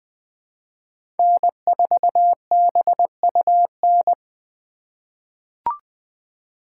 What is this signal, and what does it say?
Morse code, heard as a keyed tone, about 20 wpm, 700 Hz. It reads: N4BUN
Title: 2023 POTA Hunter Call Signs - 20wpm